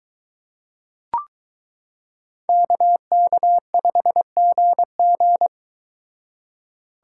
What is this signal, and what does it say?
Morse code, heard as a keyed tone, about 23 wpm, 700 Hz. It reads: KK5GG